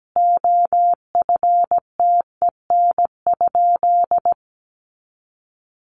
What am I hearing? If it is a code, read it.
Morse code, heard as a keyed tone, about 17 wpm, 700 Hz. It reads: OFTEN?
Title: Sentences from Top 400 Words 17wpm